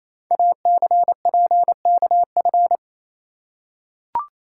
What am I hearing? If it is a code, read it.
Morse code, heard as a keyed tone, about 28 wpm, 700 Hz. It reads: ACPKF
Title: Code Groups - 5 Letter:Numbers - 28wpm